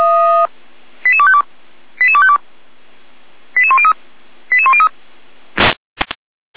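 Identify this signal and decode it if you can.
Five-tone selective call (ZVEI), heard as quick runs of five tones, tones of about 80 ms. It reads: 88252 88252 88103 88103
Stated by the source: ZVEI five-tone selective call received on an SDR.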